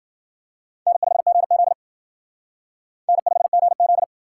Morse code, heard as a keyed tone, about 52 wpm, 700 Hz. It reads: N5GZ N5GZ